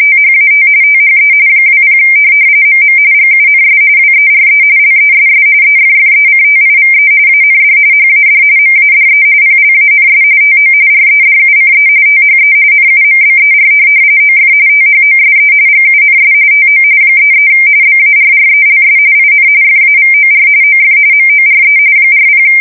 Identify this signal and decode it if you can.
RTTY, about 45.45 baud. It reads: THAT THE 20 MM GUNS WERE LESS EFFECTIVE AND HAD MADE PLANS TO REDUCE THE NUMBER OF THOSE GUNS TO 33 BARRELS IN FAVOR OF MORE 40